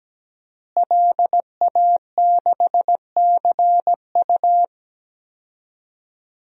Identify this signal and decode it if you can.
Morse code, heard as a keyed tone, about 17 wpm, 700 Hz. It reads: LA6CU
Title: DXCC-100 Call Signs - 5 Characters - Repeated 3x - 17wpm